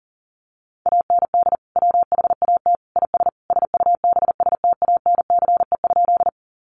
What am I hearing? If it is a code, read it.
Morse code, heard as a keyed tone, about 40 wpm, 700 Hz. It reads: AND WHAT IS SUBSTANCE?
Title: Sentences from Top 1500 Words 40wpm